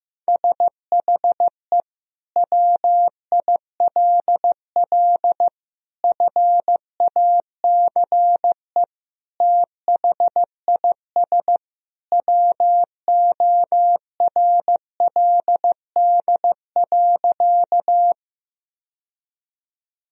Morse code, heard as a keyed tone, about 15 wpm, 700 Hz. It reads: SHE WILL FACE THIS WORLD.
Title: Sentences from Top 1000 Words - Review Part 2 - 15wpm